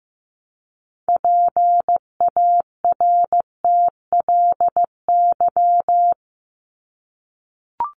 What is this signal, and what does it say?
Morse code, heard as a keyed tone, about 15 wpm, 700 Hz. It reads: PARTLY
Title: LY Suffix - Top 100 Words with Base Words 15wpm